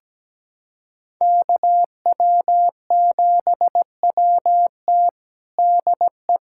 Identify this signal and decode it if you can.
Morse code, heard as a keyed tone, about 17 wpm, 700 Hz. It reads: KW7WT DE
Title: USA Callsign DE Callsign - 5 Characters - 17wpm